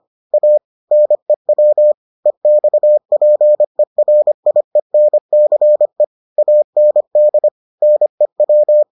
Morse code, heard as a keyed tone, about 25 wpm, 600 Hz. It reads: A NEW EXPERIENCE AND NEW